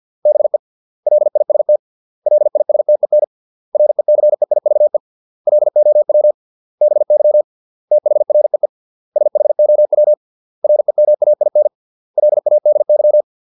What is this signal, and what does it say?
Morse code, heard as a keyed tone, about 50 wpm, 600 Hz. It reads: BE LIST LISTEN RECEIVE LOW BY THREE SHOW REMAIN LADY